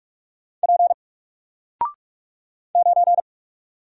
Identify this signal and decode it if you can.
Morse code, heard as a keyed tone, about 44 wpm, 700 Hz. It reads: P 9
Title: Single Letter-Number Alt 45wpm